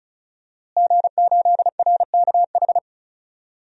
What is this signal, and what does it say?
Morse code, heard as a keyed tone, about 36 wpm, 700 Hz. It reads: G8RKH
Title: DXCC-100 Call Signs - 5 Characters - Repeated 2x - 35wpm